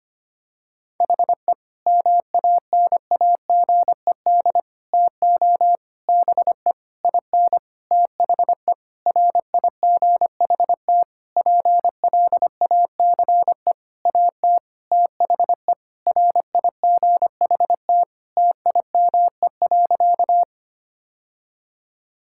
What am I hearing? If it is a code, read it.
Morse code, heard as a keyed tone, about 25 wpm, 700 Hz. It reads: HE MANAGED TO BE IN THE RIGHT PLACE AT THE RIGHT TIME.
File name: practice / 25wpm / Sentences with 9-Word Phrases - 25wpm